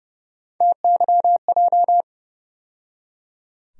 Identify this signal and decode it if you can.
Morse code, heard as a keyed tone, about 30 wpm, 700 Hz. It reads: TYJ
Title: USA Call Sign Postfixes - 3 Letters 30wpm